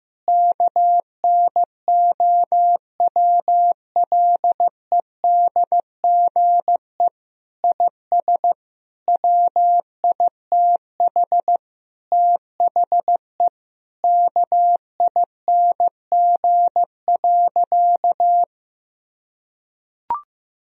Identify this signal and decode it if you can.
Morse code, heard as a keyed tone, about 16 wpm, 700 Hz. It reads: KNOWLEDGE IS WITH THE KING.
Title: Sentences from Top 1000 Words - Review Part 3 - 15wpm